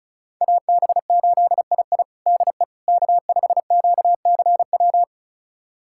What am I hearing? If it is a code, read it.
Morse code, heard as a keyed tone, about 36 wpm, 700 Hz. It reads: AB8II DE K5QCW
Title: USA Callsign DE Callsign - 5 Characters - 35wpm